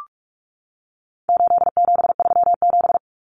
Morse code, heard as a keyed tone, about 44 wpm, 700 Hz. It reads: 8737
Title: Common Cut Numbers - 4-Digit 45wpm